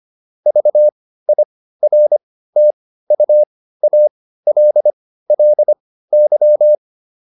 Morse code, heard as a keyed tone, about 25 wpm, spaced as for 15 wpm, 600 Hz. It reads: VIRTUALLY